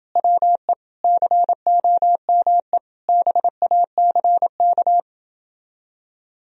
Morse code, heard as a keyed tone, about 26 wpm, 700 Hz. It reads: WE COME BACK